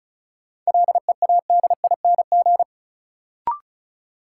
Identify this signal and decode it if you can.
Morse code, heard as a keyed tone, about 34 wpm, 700 Hz. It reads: LEADING
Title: Sentences from Top 1100 Words - New Word Drill Alt - 35wpm